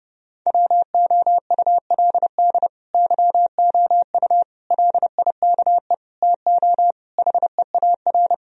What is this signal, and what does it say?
Morse code, heard as a keyed tone, about 30 wpm, 700 Hz. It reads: WOULD YOU LIKE TO HEAR
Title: Sentences from Top 1000 Words - Review Part 1 - 30wpm